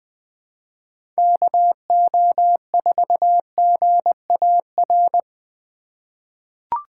Morse code, heard as a keyed tone, about 20 wpm, 700 Hz. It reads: KO4GAR
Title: 2023 POTA Hunter Call Signs - 20wpm